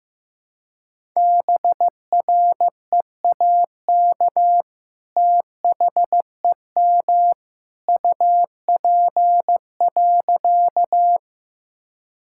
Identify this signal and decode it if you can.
Morse code, heard as a keyed tone, about 15 wpm, 700 Hz. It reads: BREAK THEM UP.
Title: Sentences from Top 1000 Words - Review Part 6 - 15wpm